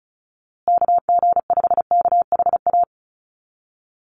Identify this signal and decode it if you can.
Morse code, heard as a keyed tone, about 34 wpm, 700 Hz. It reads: KG5KHA
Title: USA Call Signs Repeated 3x 35wpm